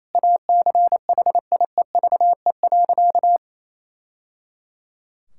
Morse code, heard as a keyed tone, about 28 wpm, 700 Hz. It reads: ACHIEVE.
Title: Sentences from Top 1000 Words - Review Part 6 - 28wpm